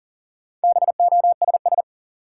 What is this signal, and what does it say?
Morse code, heard as a keyed tone, about 40 wpm, 700 Hz. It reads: BOSS
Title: Sentences from Top 700 Words - New Word Drill - 40wpm